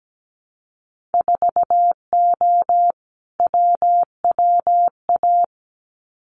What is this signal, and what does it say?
Morse code, heard as a keyed tone, about 17 wpm, 700 Hz. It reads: 4O WWA
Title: Winter Field Day Exchanges 17wpm